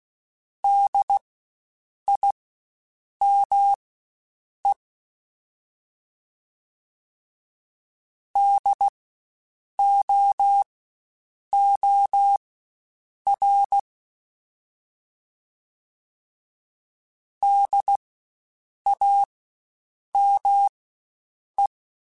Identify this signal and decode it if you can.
Morse code, heard as a keyed tone, about 16 wpm, spaced as for 7 wpm, 800 Hz. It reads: DIME DOOR DAME
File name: Spaced Alphas / Spaced Alphas mp3s / Ds sp @15